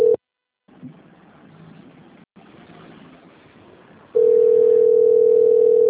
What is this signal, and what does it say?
Ringback tone (North America), 440 + 480 Hz.